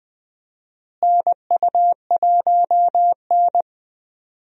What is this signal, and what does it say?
Morse code, heard as a keyed tone, about 20 wpm, 700 Hz. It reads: NU1N
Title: USA Call Signs - 4 Characters - Repeated 3x - 20wpm